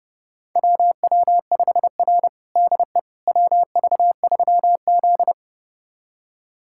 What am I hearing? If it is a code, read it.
Morse code, heard as a keyed tone, about 30 wpm, 700 Hz. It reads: WW5R DE WV3Z
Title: USA Callsign DE Callsign - 4 Characters - 30wpm